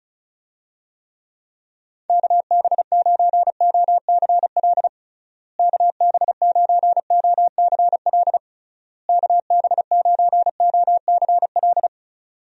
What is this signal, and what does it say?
Morse code, heard as a keyed tone, about 34 wpm, 700 Hz. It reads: KB9OCL KB9OCL KB9OCL
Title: USA Call Signs - 6 Characters - Repeated 3x - 35wpm